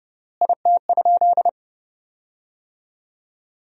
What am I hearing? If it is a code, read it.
Morse code, heard as a keyed tone, about 30 wpm, 700 Hz. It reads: IT?
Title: 3x Word-Spacing - Sentences from Top 100 Words - 30wpm